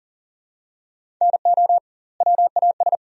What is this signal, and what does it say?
Morse code, heard as a keyed tone, about 40 wpm, 700 Hz. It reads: NO WAS
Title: 2x Word-Spacing - Sentences from Top 100 Words - 40wpm